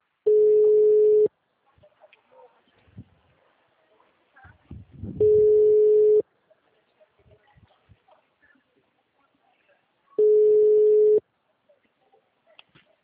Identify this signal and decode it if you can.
Ringback tone (Europe, 425 Hz).